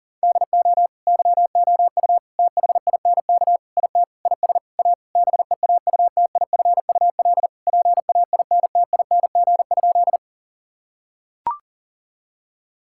Morse code, heard as a keyed tone, about 40 wpm, 700 Hz. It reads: DO YOU THINK IT IS A BEAUTIFUL PAINTING?
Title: Sentences from Top 1000 Words - Review Part 6 - 40wpm